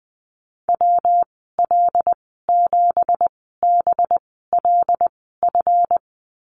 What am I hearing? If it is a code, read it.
Morse code, heard as a keyed tone, about 20 wpm, spaced as for 14 wpm, 700 Hz. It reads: WL7BLF
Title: 186 Lesson 30g - Letter b - Call Signs 20wpm